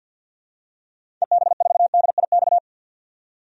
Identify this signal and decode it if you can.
Morse code, heard as a keyed tone, about 50 wpm, 700 Hz. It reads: EB4DIX
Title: DXCC-100 Call Signs - 6 Characters - 50wpm